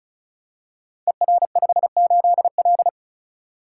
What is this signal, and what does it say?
Morse code, heard as a keyed tone, about 36 wpm, 700 Hz. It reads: ER58L
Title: Code Groups - 5 Letter:Numbers - 35wpm